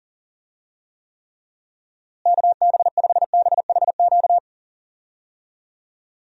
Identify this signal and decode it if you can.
Morse code, heard as a keyed tone, about 40 wpm, 700 Hz. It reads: KB5BHQ
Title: USA Call Signs - 6 Characters - Repeated 3x - 40wpm